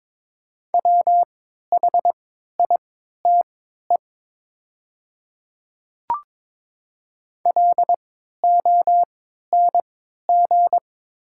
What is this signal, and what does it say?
Morse code, heard as a keyed tone, about 22 wpm, spaced as for 12 wpm, 700 Hz. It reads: WHITE LONG
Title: 3x Character Spacing - Top 300 Words - 22wpm